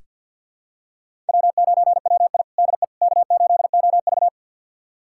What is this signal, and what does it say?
Morse code, heard as a keyed tone, about 47 wpm, 700 Hz. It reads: W9WI DE K8OV
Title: USA Callsign DE Callsign - 4 Characters - 50wpm